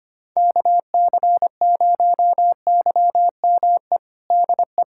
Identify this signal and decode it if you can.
Morse code, heard as a keyed tone, about 25 wpm, 700 Hz. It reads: KC0YME DE
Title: USA Callsign DE Callsign - 6 Characters - 25wpm